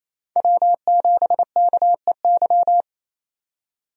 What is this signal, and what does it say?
Morse code, heard as a keyed tone, about 28 wpm, 700 Hz. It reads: W7KEY
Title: USA Call Signs - 5 Characters - 28wpm